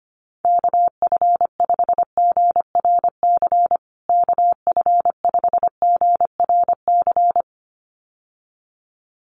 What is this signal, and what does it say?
Morse code, heard as a keyed tone, about 25 wpm, 700 Hz. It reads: KF5GRC KF5GRC
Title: USA Call Signs Repeated 2x 25wpm